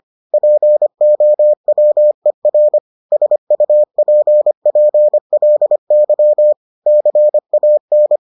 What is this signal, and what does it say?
Morse code, heard as a keyed tone, about 25 wpm, 600 Hz. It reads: POWER SUPPLY CAN